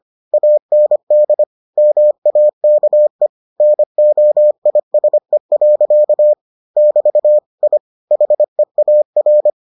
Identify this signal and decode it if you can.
Morse code, heard as a keyed tone, about 25 wpm, 600 Hz. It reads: AND MAKE NOISE. <BT> I HEAR